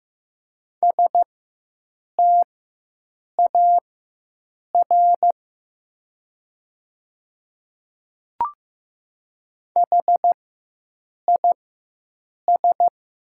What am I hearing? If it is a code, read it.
Morse code, heard as a keyed tone, about 15 wpm, spaced as for 7 wpm, 700 Hz. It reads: STAR HIS
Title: 4x Character Spacing - Top 500 Words - 15wpm